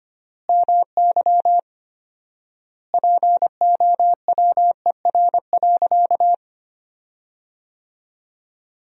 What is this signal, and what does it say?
Morse code, heard as a keyed tone, about 25 wpm, 700 Hz. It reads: MY POWER.
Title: 4x Word-Spacing - Sentences from Top 500 Words - 25wpm